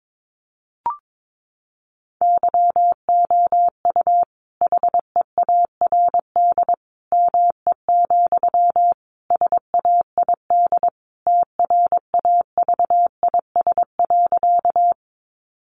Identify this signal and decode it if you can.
Morse code, heard as a keyed tone, about 22 wpm, 700 Hz. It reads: YOU HEARD ME, SAID TRAVIS.
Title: A Sound Of Thunder 22wpm